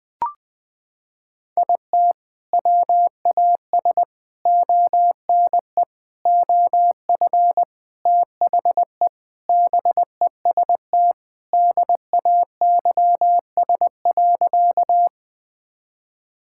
Morse code, heard as a keyed tone, about 21 wpm, 700 Hz. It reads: IT WAS ONE OF THE BEST DAYS.
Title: Sentences with 3-Word Phrases - 20wpm